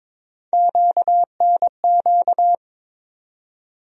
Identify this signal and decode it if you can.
Morse code, heard as a keyed tone, about 22 wpm, 700 Hz. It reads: QNQ
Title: USA Call Sign Postfixes - 3 Letters 22wpm